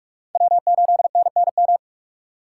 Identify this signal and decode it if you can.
Morse code, heard as a keyed tone, about 46 wpm, 700 Hz. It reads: W8NNM
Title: USA Call Signs - 5 Characters - 3x Character Spacing - 45wpm